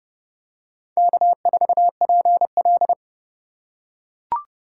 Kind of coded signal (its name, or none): Morse code